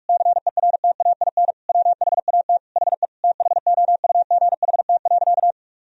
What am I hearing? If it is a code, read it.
Morse code, heard as a keyed tone, about 41 wpm, 700 Hz. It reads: CERTAIN WHAT HE THOUGHT.
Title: Sentences from Top 1000 Words - Review Part 2 - 45wpm